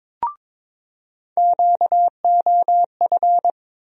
Morse code, heard as a keyed tone, about 22 wpm, 700 Hz. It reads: QOF
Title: USA Call Sign Postfixes - 3 Letters 22wpm